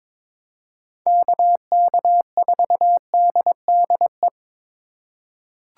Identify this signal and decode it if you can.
Morse code, heard as a keyed tone, about 22 wpm, 700 Hz. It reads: KK4DDE
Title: USA Call Signs - 6 Characters - 4x Character Spacing - 22wpm